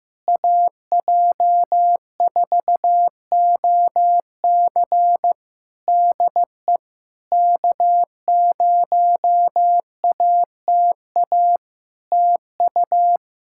Morse code, heard as a keyed tone, about 15 wpm, 700 Hz. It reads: AJ4OC DE K0ATA TU
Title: Practice QSO - Exchange 4 - Ending - 15wpm